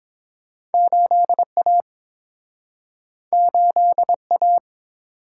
Morse code, heard as a keyed tone, about 30 wpm, 700 Hz. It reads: 8A 8A